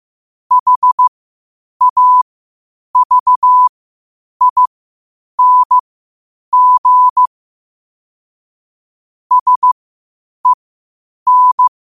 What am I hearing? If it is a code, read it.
Morse code, heard as a keyed tone, about 15 wpm, spaced as for 8 wpm, 1000 Hz. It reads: HAVING SEN
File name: qotdfiles / 8wpm